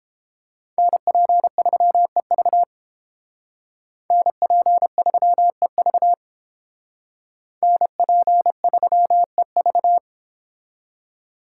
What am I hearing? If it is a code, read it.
Morse code, heard as a keyed tone, about 30 wpm, 700 Hz. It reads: NP3EV NP3EV NP3EV